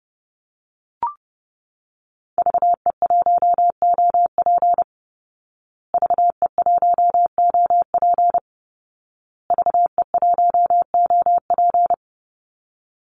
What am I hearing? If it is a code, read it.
Morse code, heard as a keyed tone, about 30 wpm, 700 Hz. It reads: VE1OP VE1OP VE1OP
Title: Call Signs Repeated 2x - 30wpm